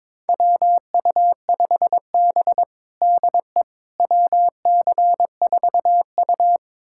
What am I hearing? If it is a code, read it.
Morse code, heard as a keyed tone, about 22 wpm, 700 Hz. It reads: WU5B DE WC4U
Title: USA Callsign DE Callsign - 4 Characters - 22wpm